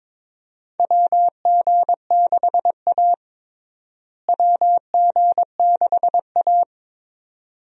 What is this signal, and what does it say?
Morse code, heard as a keyed tone, about 22 wpm, 700 Hz. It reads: WG6A WG6A
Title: USA Call Signs - 4 Characters - Repeated 2x - 22wpm